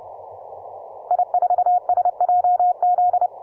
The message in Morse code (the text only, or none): I4SJZ